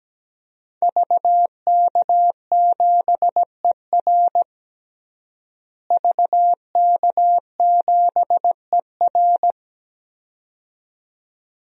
Morse code, heard as a keyed tone, about 17 wpm, 700 Hz. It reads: VK7ER VK7ER
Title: DXCC-100 Call Signs - 5 Characters - Repeated 2x - 17wpm